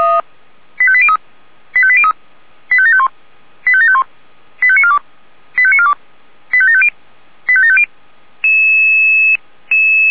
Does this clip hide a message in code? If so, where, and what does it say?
Yes: five-tone selective call (ZVEI) from 0.8 to 7.8 s, heard as quick runs of five tones, tones of about 70 ms. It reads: 85803 85803 85751 85751 85942 85942 85750 85750